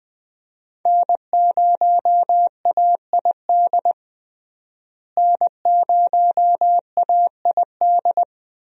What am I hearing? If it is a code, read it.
Morse code, heard as a keyed tone, about 20 wpm, 700 Hz. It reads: N0AID N0AID
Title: USA Call Signs - 5 Characters - Repeated 2x - 20wpm